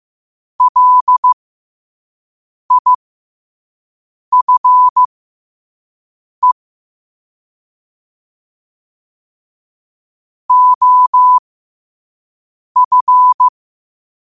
Morse code, heard as a keyed tone, about 15 wpm, spaced as for 5 wpm, 1000 Hz. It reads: LIFE OF
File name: qotdfiles / 5wpm